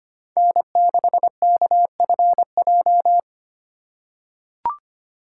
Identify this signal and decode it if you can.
Morse code, heard as a keyed tone, about 25 wpm, 700 Hz. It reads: N6KFJ
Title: USA Call Signs - 5 Characters - 2x Character Spacing - 25wpm